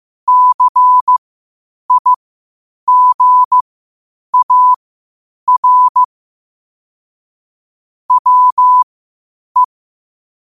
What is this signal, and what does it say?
Morse code, heard as a keyed tone, about 15 wpm, spaced as for 8 wpm, 1000 Hz. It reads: CIGAR WE